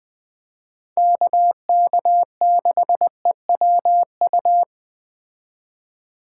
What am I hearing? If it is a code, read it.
Morse code, heard as a keyed tone, about 19 wpm, 700 Hz. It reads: KK6EWU